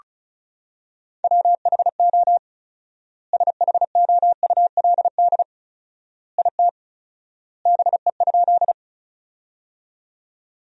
Morse code, heard as a keyed tone, about 36 wpm, 700 Hz. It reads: WHO SHOULD IT BE?
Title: 4x Word-Spacing - Sentences from Top 300 Words - 35wpm